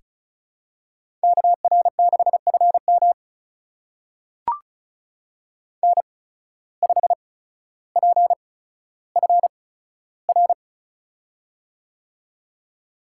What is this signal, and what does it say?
Morse code, heard as a keyed tone, about 36 wpm, spaced as for 15 wpm, 700 Hz. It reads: KR6FM N5PFR
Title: USA Call Signs - 5 Characters - 8x Character Spacing - 35wpm